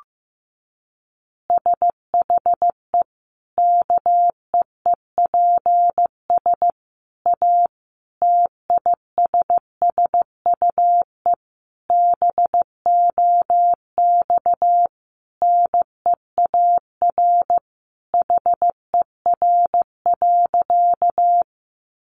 Morse code, heard as a keyed tone, about 15 wpm, 700 Hz. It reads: SHE KEEPS A TISSUE BOX NEAR HER.
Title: Easy Phrases 15wpm